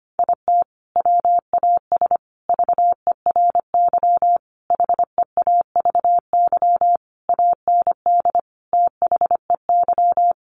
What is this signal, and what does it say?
Morse code, heard as a keyed tone, about 25 wpm, 700 Hz. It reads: IT WAS VERY HEAVY AND THEY
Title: Sentences from Top 1100 Words 25wpm